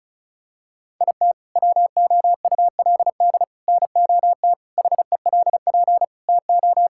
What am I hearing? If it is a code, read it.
Morse code, heard as a keyed tone, about 36 wpm, 700 Hz. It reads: IT WOULD NOT HELP TO
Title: Sentences from Top 1600 Words 35wpm